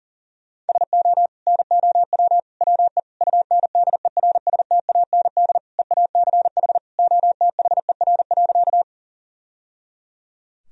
Morse code, heard as a keyed tone, about 37 wpm, 700 Hz. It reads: SO NOW WE UNDERSTAND EACH OTHER.
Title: Sentences from Top 1000 Words - Review Part 1 - 40wpm